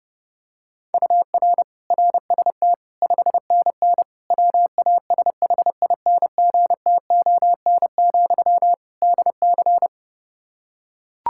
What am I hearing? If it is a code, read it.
Morse code, heard as a keyed tone, about 32 wpm, 700 Hz. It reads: UR RST 5NN WASHINGTON, DC
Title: Practice QSO - Exchange 1 - Beginning - 30wpm